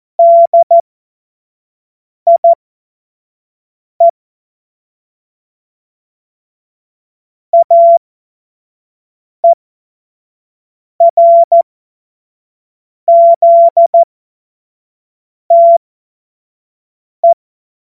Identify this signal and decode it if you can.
Morse code, heard as a keyed tone, about 14 wpm, spaced as for 5 wpm, 680 Hz. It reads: DIE AERZTE
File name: Morsekurs / Witze / Witz 19 März 2024 25 F70